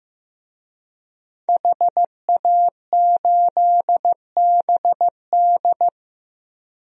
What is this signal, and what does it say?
Morse code, heard as a keyed tone, about 15 wpm, 700 Hz. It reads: HA8BD
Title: DXCC-100 Call Signs - 5 Characters - Repeated 2x - 15wpm